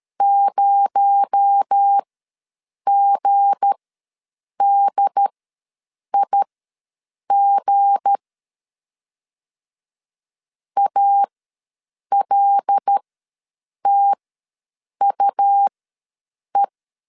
Morse code, heard as a keyed tone, about 13 wpm, spaced as for 7 wpm, 800 Hz. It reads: 0GDIG ALTUE
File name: LEZIONI / 7 GF9Z / 7 G ws 12wpm 3140